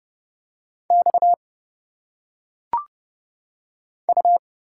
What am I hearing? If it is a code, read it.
Morse code, heard as a keyed tone, about 31 wpm, 700 Hz. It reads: X U
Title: Single Letters - Phonetic Pronunciation 30wpm